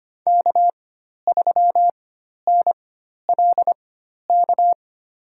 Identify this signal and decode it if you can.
Morse code, heard as a keyed tone, about 25 wpm, spaced as for 12 wpm, 700 Hz. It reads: K3NLK
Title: USA Call Signs - 5 Characters - 4x Character Spacing - 25wpm